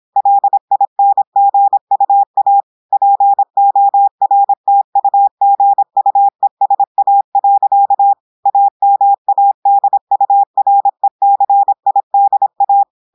Morse code, heard as a keyed tone, about 26 wpm, 800 Hz. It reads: LINGUA PORTUGUESA. AMADURECIDA
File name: audios / portugais / portugais texte en clair 1 vitesse 26 mots minutes